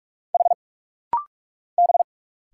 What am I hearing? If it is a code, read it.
Morse code, heard as a keyed tone, about 46 wpm, 700 Hz. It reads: H B
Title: Single Letters - Rapid-Fire 45wpm